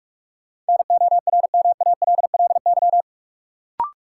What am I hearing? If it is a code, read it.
Morse code, heard as a keyed tone, about 43 wpm, 700 Hz. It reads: NORMALLY
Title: Sentences from Top 1000 Words - Review Part 6 - New Word Drill - 45wpm